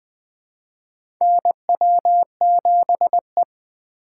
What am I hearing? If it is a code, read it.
Morse code, heard as a keyed tone, about 20 wpm, 700 Hz. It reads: NW7E
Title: 2023 SOTA Chaser Call Signs - 20wpm